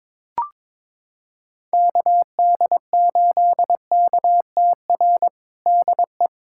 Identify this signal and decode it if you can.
Morse code, heard as a keyed tone, about 22 wpm, 700 Hz. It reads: KD8KTR DE
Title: USA Callsign DE Callsign - 6 Characters - 22wpm